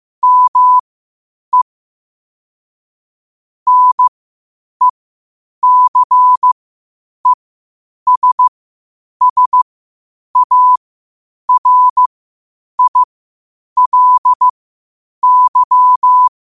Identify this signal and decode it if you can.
Morse code, heard as a keyed tone, about 15 wpm, spaced as for 8 wpm, 1000 Hz. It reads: ME NECESSARILY
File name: qotdfiles / 8wpm